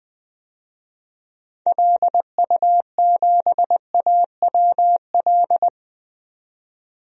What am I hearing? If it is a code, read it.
Morse code, heard as a keyed tone, about 20 wpm, 700 Hz. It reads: LU7AWL
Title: DXCC-100 Call Signs - 6 Characters - 20wpm